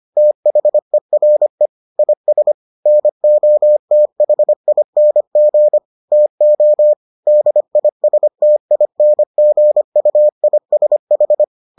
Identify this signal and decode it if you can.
Morse code, heard as a keyed tone, about 25 wpm, 600 Hz. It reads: THERE IS NOTHING TO DISTINGUISH